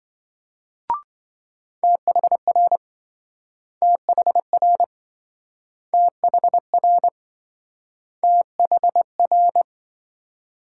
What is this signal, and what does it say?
Morse code, heard as a keyed tone, about 27 wpm, 700 Hz. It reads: THR THR THR THR